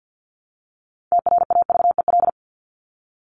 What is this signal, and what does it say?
Morse code, heard as a keyed tone, about 50 wpm, 700 Hz. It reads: TRAVEL
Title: Top 500 Words 50wpm